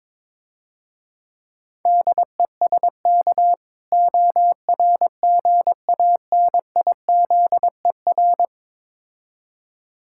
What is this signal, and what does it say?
Morse code, heard as a keyed tone, about 22 wpm, 700 Hz. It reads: DESK ORGANIZER
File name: Easy Words - Speed-Racing with Hints - 22wpm